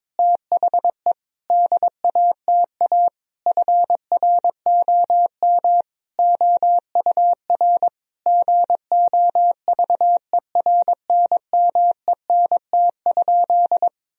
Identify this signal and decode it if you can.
Morse code, heard as a keyed tone, about 22 wpm, 700 Hz. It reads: THE DATA FROM OUR GOVERNMENT?
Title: Sentences from Top 1000 Words - Review Part 3 - 22wpm